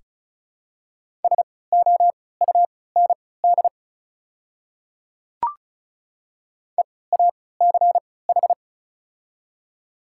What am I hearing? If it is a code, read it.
Morse code, heard as a keyed tone, about 34 wpm, spaced as for 20 wpm, 700 Hz. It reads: SOUND EACH
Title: 3x Character Spacing - Top 100 Words - 35wpm